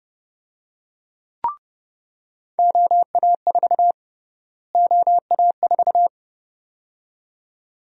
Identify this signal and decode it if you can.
Morse code, heard as a keyed tone, about 30 wpm, 700 Hz. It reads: OA4 OA4
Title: USA Call Sign Prefixes - Repeated 2x - 30wpm